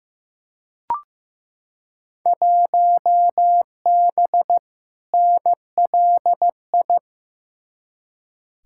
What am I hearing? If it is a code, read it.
Morse code, heard as a keyed tone, about 15 wpm, 700 Hz. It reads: 1B NLI
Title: Field Day Exchanges Encore - 15wpm